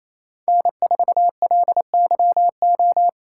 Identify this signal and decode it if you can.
Morse code, heard as a keyed tone, about 36 wpm, 700 Hz. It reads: N4LYO